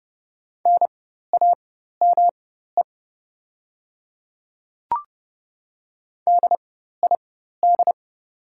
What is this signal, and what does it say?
Morse code, heard as a keyed tone, about 30 wpm, spaced as for 14 wpm, 700 Hz. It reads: NAME DID
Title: 4x Character Spacing - Top 200 Words - 30wpm